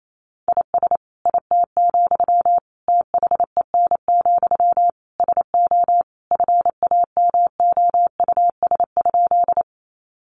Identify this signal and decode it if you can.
Morse code, heard as a keyed tone, about 28 wpm, 700 Hz. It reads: IS IT, THEN, SO FAMOUS?
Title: Sentences from Top 1500 Words 28wpm